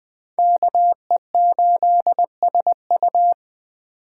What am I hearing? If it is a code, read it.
Morse code, heard as a keyed tone, about 24 wpm, 700 Hz. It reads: KE8SU
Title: Speed Racing - USA Call Signs 20wpm